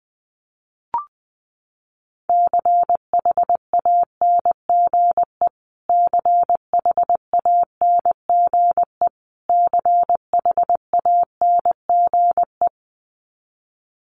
Morse code, heard as a keyed tone, about 20 wpm, 700 Hz. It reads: CHANGE CHANGE CHANGE
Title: Top 200 Words Repeated 20wpm